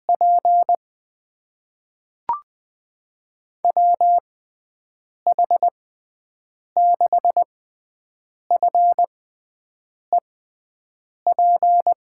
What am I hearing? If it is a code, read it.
Morse code, heard as a keyed tone, about 20 wpm, spaced as for 7 wpm, 700 Hz. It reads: P WH6FEP
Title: USA Call Signs - 6 Characters - 6x Character Spacing - 20wpm